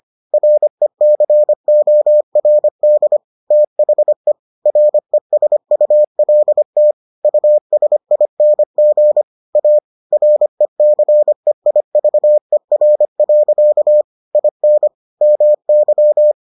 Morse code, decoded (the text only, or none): RECORD THE RESULT USING A RECEIVER. IN MY